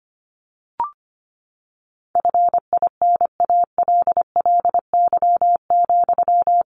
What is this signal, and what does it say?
Morse code, heard as a keyed tone, about 25 wpm, 700 Hz. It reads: FINALLY,
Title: Sentences from Top 1300 Words 25wpm